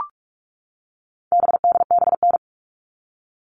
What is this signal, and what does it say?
Morse code, heard as a keyed tone, about 44 wpm, 700 Hz. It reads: BDBN
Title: All Cut Numbers - 4-Digit 45wpm